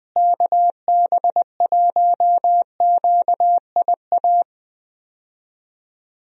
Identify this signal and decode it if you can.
Morse code, heard as a keyed tone, about 20 wpm, 700 Hz. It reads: KB1QIA
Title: USA Call Signs - 6 Characters - Repeated 3x - 20wpm